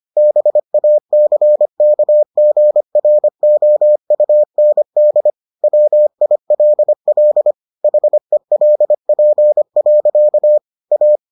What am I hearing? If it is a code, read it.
Morse code, heard as a keyed tone, about 25 wpm, 600 Hz. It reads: BACKGROUND WILL HELP. A